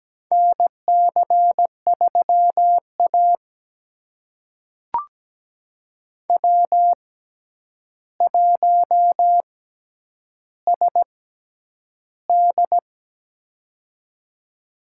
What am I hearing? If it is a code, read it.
Morse code, heard as a keyed tone, about 17 wpm, spaced as for 9 wpm, 700 Hz. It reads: NC3A W1SD
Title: USA Call Signs - 4 Characters - 6x Character Spacing - 17wpm